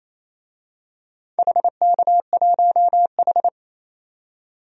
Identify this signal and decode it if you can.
Morse code, heard as a keyed tone, about 27 wpm, 700 Hz. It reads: HK1H
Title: DXCC-100 Call Signs - 4 Characters - 28wpm